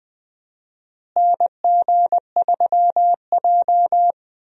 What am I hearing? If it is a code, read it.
Morse code, heard as a keyed tone, about 20 wpm, 700 Hz. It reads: NG3J
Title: USA Call Signs - 4 Characters - Repeated 2x - 20wpm